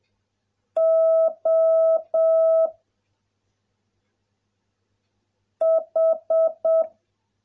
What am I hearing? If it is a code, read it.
Morse code, heard as a keyed tone, about 7 wpm, 660 Hz. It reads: O H